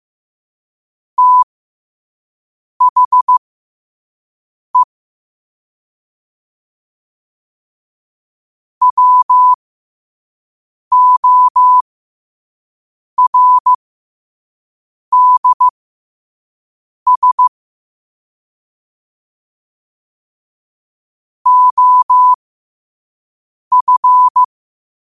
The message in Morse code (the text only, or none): THE WORDS OF